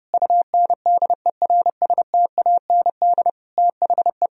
Morse code, heard as a keyed tone, about 30 wpm, 700 Hz. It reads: UNDERSTAND THE
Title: Sentences from Top 1000 Words - Review Part 2 - 30wpm